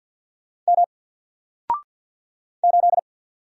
Morse code, heard as a keyed tone, about 50 wpm, 700 Hz. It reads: M 8
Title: Single Character (Letter:Number:Punctuation:Prosign) 50wpm